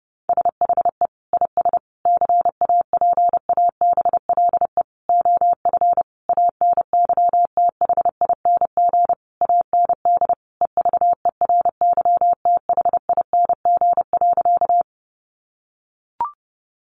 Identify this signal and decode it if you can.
Morse code, heard as a keyed tone, about 30 wpm, 700 Hz. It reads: SHE IS CAPABLE OF ANYTHING AND EVERYTHING.
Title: Sentences from Top 1200 Words 30wpm